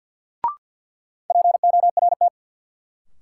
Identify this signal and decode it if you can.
Morse code, heard as a keyed tone, about 52 wpm, 700 Hz. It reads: PORT
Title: Sentences from Top 1000 Words - Review Part 1 - New Word Drill - 50wpm